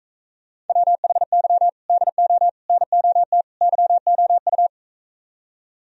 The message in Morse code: WHY DO NOT YOU

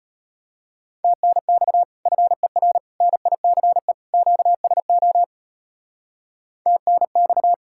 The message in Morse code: TNX FER NICE QSO TNX

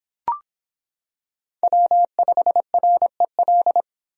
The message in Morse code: W5REL

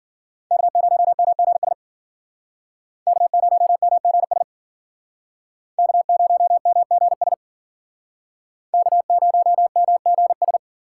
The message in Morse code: K0MGS K0MGS K0MGS K0MGS

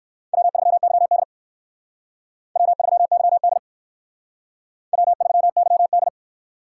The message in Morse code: W3YD W3YD W3YD

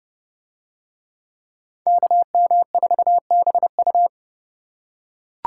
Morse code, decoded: KM4BU

